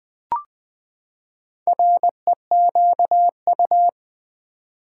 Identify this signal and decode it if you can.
Morse code, heard as a keyed tone, about 20 wpm, 700 Hz. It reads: REQU
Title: Build A Word - 4 Letters At A Time - 11-Letter Words - 20wpm